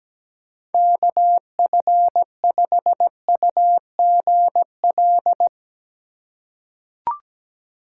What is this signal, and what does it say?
Morse code, heard as a keyed tone, about 17 wpm, 700 Hz. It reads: KF5UGL
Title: USA Call Signs - 2x Character Spacing - 17wpm